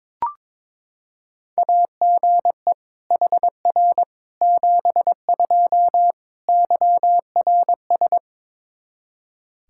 Morse code, heard as a keyed tone, about 22 wpm, 700 Hz. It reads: AGE HR 72 YRS
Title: Practice QSO - Exchange 3 - More Info - 22wpm